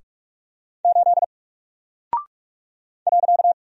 Morse code, heard as a keyed tone, about 44 wpm, 700 Hz. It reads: 8 .
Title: Single Character (Letter:Number:Punctuation:Prosign) 45wpm